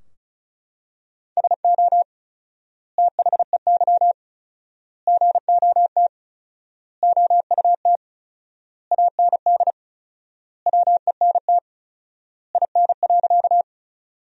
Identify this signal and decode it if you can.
Morse code, heard as a keyed tone, about 34 wpm, 700 Hz. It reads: SO THEY GOT OUT AND WENT IN.
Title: 4x Word-Spacing - Sentences from Top 300 Words - 35wpm